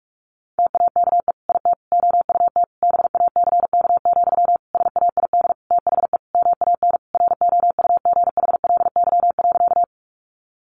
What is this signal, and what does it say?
Morse code, heard as a keyed tone, about 44 wpm, 700 Hz. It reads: TAKE IT OUT BACK, SAID THE MAN ROUGHLY.
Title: Sentences from Top 1500 Words 45wpm